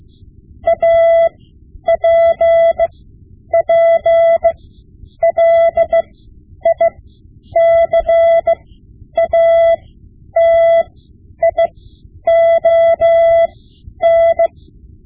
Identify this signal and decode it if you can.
Morse code, heard as a keyed tone, about 16 wpm, spaced as for 9 wpm, 670 Hz. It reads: APPLICATION
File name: Morse Lessons / Morse Lessons / Tape-5B